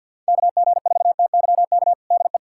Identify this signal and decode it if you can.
Morse code, heard as a keyed tone, about 50 wpm, 700 Hz. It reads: KG4TYK DE
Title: USA Callsign DE Callsign - 6 Characters - 50wpm